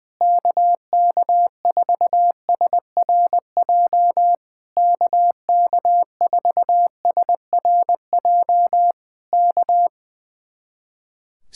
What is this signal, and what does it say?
Morse code, heard as a keyed tone, about 19 wpm, 700 Hz. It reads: KK4SRJ KK4SRJ K
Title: Speed-Racing Common QSO Phrases - 20wpm